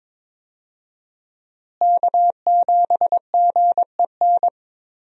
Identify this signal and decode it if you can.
Morse code, heard as a keyed tone, about 22 wpm, 700 Hz. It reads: K7GEN
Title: USA Call Signs - 5 Characters - Repeated 3x - 22wpm